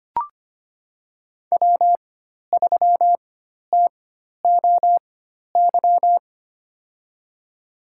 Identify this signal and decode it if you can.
Morse code, heard as a keyed tone, about 25 wpm, spaced as for 12 wpm, 700 Hz. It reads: W3TOY
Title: USA Call Signs - 4x Character Spacing - 25wpm